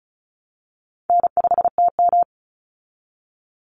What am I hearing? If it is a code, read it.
Morse code, heard as a keyed tone, about 34 wpm, 700 Hz. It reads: N5TM
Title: Call Signs Repeated 3x - 35wpm